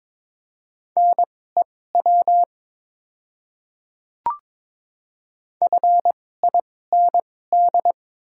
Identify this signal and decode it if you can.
Morse code, heard as a keyed tone, about 22 wpm, spaced as for 16 wpm, 700 Hz. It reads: NEW FIND
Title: 2x Character Spacing - Top 300 Words - 22wpm